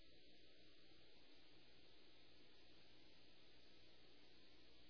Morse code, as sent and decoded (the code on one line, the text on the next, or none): none